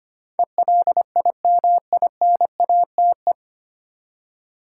Morse code, heard as a keyed tone, about 25 wpm, 700 Hz. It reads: ELIMINATE
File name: Nine Letter Words - 25wpm